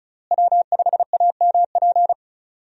Morse code, heard as a keyed tone, about 36 wpm, 700 Hz. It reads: W5AMP
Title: USA Call Signs - 5 Characters - Repeated 2x - 35wpm